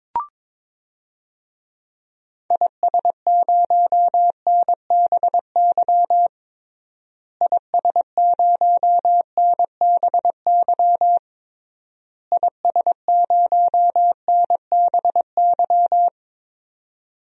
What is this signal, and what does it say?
Morse code, heard as a keyed tone, about 22 wpm, 700 Hz. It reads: IS0NBY IS0NBY IS0NBY
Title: DXCC-100 Call Signs - 6 Characters - Repeated 3x - 22wpm